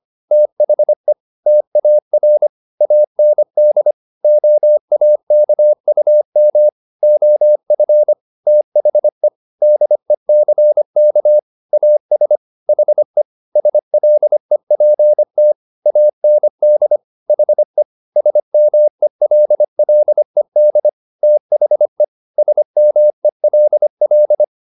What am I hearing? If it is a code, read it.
Morse code, heard as a keyed tone, about 25 wpm, 600 Hz. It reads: THE TAR AND OAKUM OF THE DECK AS HE SLEPT AND HE SMELLED THE SMELL
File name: the old man and the sea part 1 25wpm